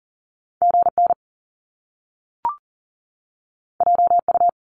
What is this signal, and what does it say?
Morse code, heard as a keyed tone, about 42 wpm, 700 Hz. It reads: GN JU
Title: Top Two Letter Combinations in Words 40wpm